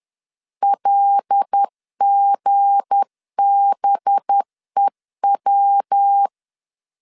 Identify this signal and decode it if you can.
Morse code, heard as a keyed tone, about 10 wpm, 800 Hz. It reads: LGBEW
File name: LEZIONI / 7 GF9Z / 7 GF9Z 10wpm 2611